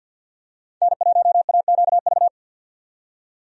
Morse code, heard as a keyed tone, about 47 wpm, 700 Hz. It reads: N1AQV
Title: USA Call Signs - 5 Characters - 50wpm